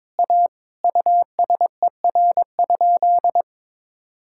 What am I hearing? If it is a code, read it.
Morse code, heard as a keyed tone, about 22 wpm, 700 Hz. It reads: A USER?
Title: Sentences from Top 1000 Words - Review Part 5 - 22wpm